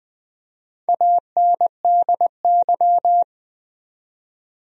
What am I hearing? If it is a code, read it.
Morse code, heard as a keyed tone, about 20 wpm, 700 Hz. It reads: ANDY
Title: USA Common Names - Max 4 Letters - 20wpm